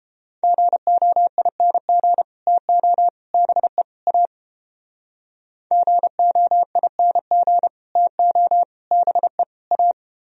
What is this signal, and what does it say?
Morse code, heard as a keyed tone, about 30 wpm, 700 Hz. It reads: GOING TO BE A GOING TO BE A